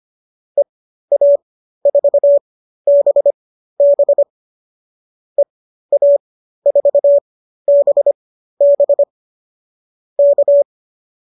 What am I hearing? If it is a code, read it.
Morse code, heard as a keyed tone, about 25 wpm, spaced as for 13 wpm, 590 Hz. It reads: EA4BB EA4BB K